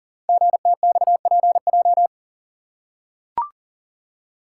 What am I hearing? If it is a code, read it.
Morse code, heard as a keyed tone, about 41 wpm, 700 Hz. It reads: GTXPJ
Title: Code Groups - 5 Letters - 40wpm